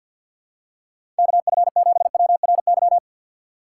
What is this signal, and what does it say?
Morse code, heard as a keyed tone, about 48 wpm, 700 Hz. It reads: KF7WRY
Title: USA Call Signs - 3x Character Spacing - 50wpm